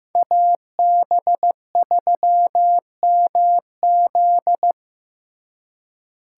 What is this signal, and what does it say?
Morse code, heard as a keyed tone, about 18 wpm, 700 Hz. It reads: AB3MZ
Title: Speed Racing - USA Call Signs 15wpm